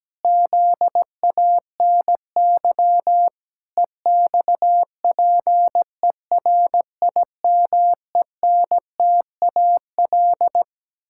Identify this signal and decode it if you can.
Morse code, heard as a keyed tone, about 17 wpm, 700 Hz. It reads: ZANY EXPERIMENTAL
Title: Pangram Challenge v2 - 17wpm